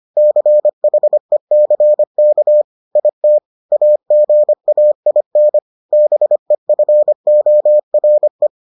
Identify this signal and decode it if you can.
Morse code, heard as a keyed tone, about 25 wpm, 600 Hz. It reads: CHECK IT AGAIN BEFORE